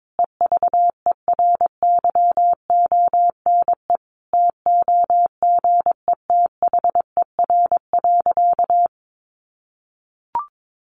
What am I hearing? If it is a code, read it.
Morse code, heard as a keyed tone, about 22 wpm, 700 Hz. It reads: EVERYONE TOGETHER.
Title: Sentences from Top 1500 Words 22wpm